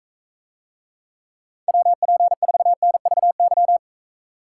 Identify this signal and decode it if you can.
Morse code, heard as a keyed tone, about 45 wpm, 700 Hz. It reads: WP4NVY